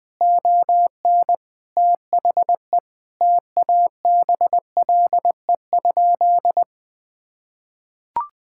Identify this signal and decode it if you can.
Morse code, heard as a keyed tone, about 20 wpm, 700 Hz. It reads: ON THE TABLE?
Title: Sentences from Top 1600 Words 20wpm